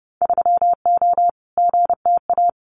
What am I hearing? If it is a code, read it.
Morse code, heard as a keyed tone, about 30 wpm, 700 Hz. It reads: 3O GTA
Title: Winter Field Day Exchanges Encore 30wpm